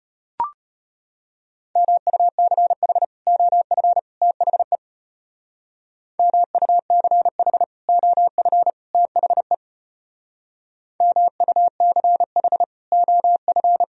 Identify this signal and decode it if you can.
Morse code, heard as a keyed tone, about 34 wpm, 700 Hz. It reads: MUCH OF THE MUCH OF THE MUCH OF